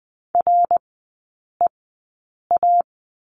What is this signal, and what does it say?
Morse code, heard as a keyed tone, about 20 wpm, spaced as for 8 wpm, 700 Hz. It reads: REA
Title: 053 Lesson 10e - Letter r - Three Characters 20wpm